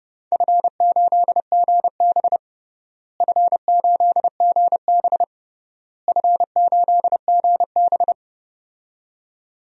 Morse code, heard as a keyed tone, about 29 wpm, 700 Hz. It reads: F8GB F8GB F8GB
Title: DXCC-100 Call Signs - 4 Characters - Repeated 3x - 30wpm